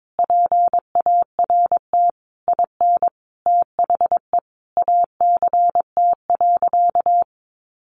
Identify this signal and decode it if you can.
Morse code, heard as a keyed tone, about 22 wpm, 700 Hz. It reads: PART IN THE ACT.
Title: Sentences from Top 1100 Words 22wpm